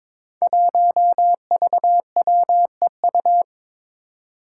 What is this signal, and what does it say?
Morse code, heard as a keyed tone, about 22 wpm, 700 Hz. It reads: 1VWEU
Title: Five Challenging Letter-Number Combinations 22wpm